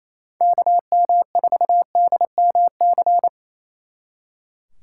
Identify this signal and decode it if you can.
Morse code, heard as a keyed tone, about 33 wpm, 700 Hz. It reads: KM4DMC
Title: Speed Racing - USA 2x3 Call Signs 28wpm